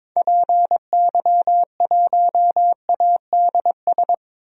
Morse code, heard as a keyed tone, about 22 wpm, 700 Hz. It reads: PY1ADS
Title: DXCC-100 Call Signs - 6 Characters - 22wpm